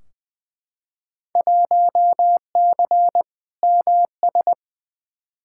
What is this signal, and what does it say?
Morse code, heard as a keyed tone, about 20 wpm, 700 Hz. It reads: 1C MS
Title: Field Day Exchanges Encore - 20wpm